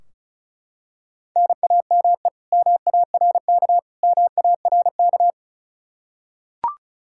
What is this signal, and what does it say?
Morse code, heard as a keyed tone, about 36 wpm, 700 Hz. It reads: NAME MARK MARK
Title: Practice QSO - Exchange 1 - Beginning - 35wpm